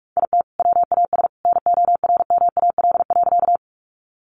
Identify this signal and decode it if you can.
Morse code, heard as a keyed tone, about 44 wpm, 700 Hz. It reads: IT WAS NORMAL.
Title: Sentences from Top 1200 Words 45wpm